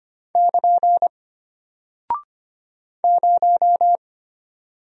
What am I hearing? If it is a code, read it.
Morse code, heard as a keyed tone, about 25 wpm, 700 Hz. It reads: <KN> 0
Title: Single Character (Letter:Number:Punctuation:Prosign) Alt 25wpm